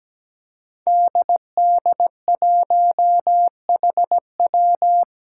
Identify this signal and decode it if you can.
Morse code, heard as a keyed tone, about 17 wpm, 700 Hz. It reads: DD1HW
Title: Five Challenging Letter-Number Combinations 17wpm